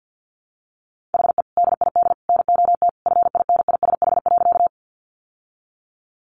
Morse code, heard as a keyed tone, about 50 wpm, 700 Hz. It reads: HE DID NOT FINISH.
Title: Sentences from Top 1200 Words 50wpm